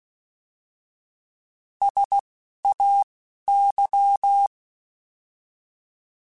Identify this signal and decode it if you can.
Morse code, heard as a keyed tone, about 16 wpm, 800 Hz. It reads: S A Y
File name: S vs H / S vs H mp3s / S vs H @10